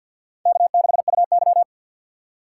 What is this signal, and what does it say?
Morse code, heard as a keyed tone, about 52 wpm, 700 Hz. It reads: K6UY
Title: USA Call Signs - 4 Characters - 3x Character Spacing - 50wpm